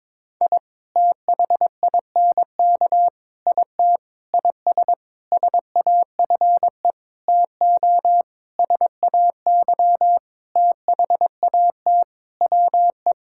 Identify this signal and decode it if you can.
Morse code, heard as a keyed tone, about 22 wpm, 700 Hz. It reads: I THINK IT IS SAFE TO SAY THAT WE
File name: practice / 22wpm / Sentences with 8-Word Phrases - 22wpm